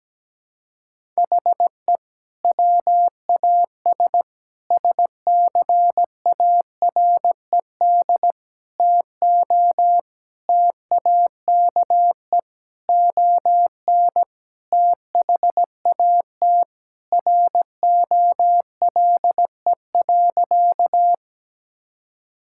Morse code, heard as a keyed tone, about 17 wpm, 700 Hz. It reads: HE WAS SCARED TO TAKE ON THAT ROLE.
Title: Sentences from Top 1000 Words - Review Part 6 - 17wpm